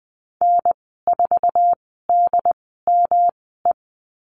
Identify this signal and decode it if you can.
Morse code, heard as a keyed tone, about 19 wpm, spaced as for 14 wpm, 700 Hz. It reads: N4DME
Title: 167 Lesson 27g - Number 7 - Call Signs 20wpm